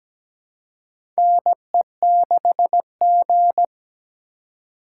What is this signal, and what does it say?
Morse code, heard as a keyed tone, about 17 wpm, 700 Hz. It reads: NE6G
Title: USA Call Signs - 4 Characters - 3x Character Spacing - 17wpm